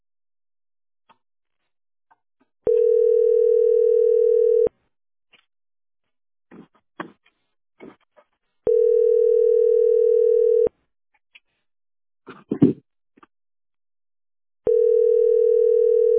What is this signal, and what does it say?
Ringback tone (North America), 440 + 480 Hz.